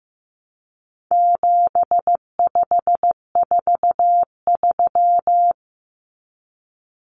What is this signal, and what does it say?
Morse code, heard as a keyed tone, about 15 wpm, 700 Hz. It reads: 7543
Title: Common Cut Numbers - 4-Digit 15wpm